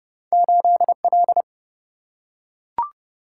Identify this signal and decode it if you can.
Morse code, heard as a keyed tone, about 30 wpm, 700 Hz. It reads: 8L
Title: Single-Letter Single-Number 30wpm